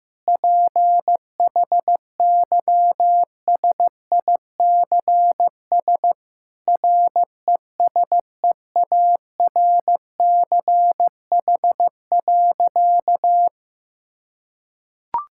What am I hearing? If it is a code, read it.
Morse code, heard as a keyed tone, about 15 wpm, 700 Hz. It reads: PHYSICS RESEARCH.
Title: Sentences from Top 1000 Words - Review Part 5 - 15wpm